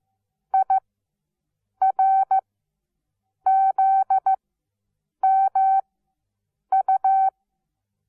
Morse code, heard as a keyed tone, about 15 wpm, spaced as for 7 wpm, 770 Hz. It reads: IRZMU
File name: Onlinelehrgang / morsen / DJ4UF Morsen17